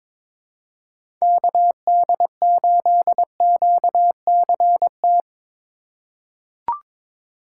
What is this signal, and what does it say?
Morse code, heard as a keyed tone, about 22 wpm, 700 Hz. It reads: KD8QCT
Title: USA Call Signs - 6 Characters - 3x Character Spacing - 22wpm